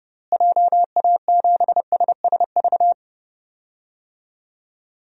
Morse code, heard as a keyed tone, about 30 wpm, 700 Hz. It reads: JA7SSV
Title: DXCC-100 Call Signs - 6 Characters - Repeated 3x - 30wpm